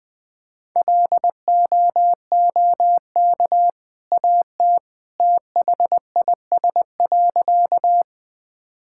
Morse code, heard as a keyed tone, about 21 wpm, 700 Hz. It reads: LOOK AT THIS.
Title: Sentences with 2-Word Phrases - Spoken Beginning - 20wpm